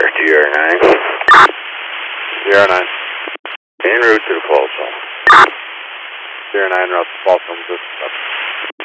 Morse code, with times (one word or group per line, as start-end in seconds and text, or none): none